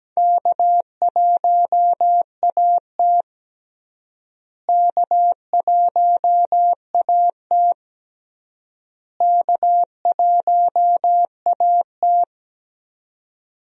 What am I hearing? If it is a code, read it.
Morse code, heard as a keyed tone, about 17 wpm, 700 Hz. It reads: K1AT K1AT K1AT
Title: USA Call Signs - 4 Characters - Repeated 3x - 17wpm